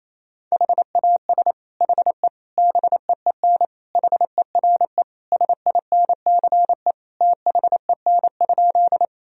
Morse code, heard as a keyed tone, about 27 wpm, 700 Hz. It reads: HAS HE BEEN HERE SINCE THEN?
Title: Sentences from Top 1000 Words - Review Part 1 - 28wpm